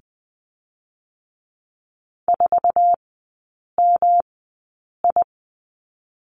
Morse code, heard as a keyed tone, about 19 wpm, spaced as for 8 wpm, 700 Hz. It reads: 4MI
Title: 218 Lesson 35e - Number 0 - Three Characters 20wpm